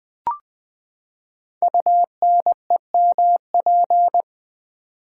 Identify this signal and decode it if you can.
Morse code, heard as a keyed tone, about 20 wpm, 700 Hz. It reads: UNEMP
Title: Build A Word - 5 Letters At A Time - 12-Letter Words - 20wpm